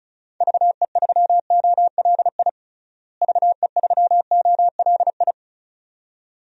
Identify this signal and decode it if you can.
Morse code, heard as a keyed tone, about 34 wpm, 700 Hz. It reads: VE3OLI VE3OLI
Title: DXCC-100 Call Signs - 6 Characters - Repeated 2x - 35wpm